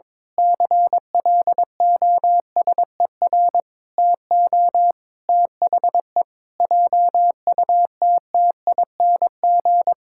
Morse code, heard as a keyed tone, about 22 wpm, 700 Hz. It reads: CLOSER TO THE JUTTING
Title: Pangram Challenge v2 - 22wpm